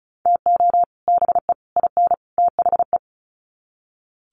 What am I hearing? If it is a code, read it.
Morse code, heard as a keyed tone, about 34 wpm, 700 Hz. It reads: TO BE IN THE
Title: Sets of 4 Words 35wpm